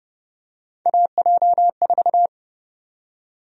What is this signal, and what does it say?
Morse code, heard as a keyed tone, about 30 wpm, 700 Hz. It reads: AJ4
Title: USA Call Sign Prefixes - 30wpm